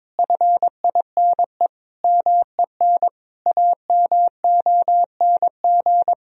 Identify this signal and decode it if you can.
Morse code, heard as a keyed tone, about 21 wpm, 700 Hz. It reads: FINE MEN AMONG
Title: Sentences from Top 1000 Words - Review Part 2 - 22wpm